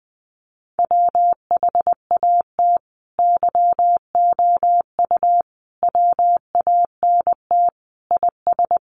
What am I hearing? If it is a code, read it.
Morse code, heard as a keyed tone, about 20 wpm, 700 Hz. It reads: WHAT YOU WANT IS
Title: Sentences from Top 700 Words 20wpm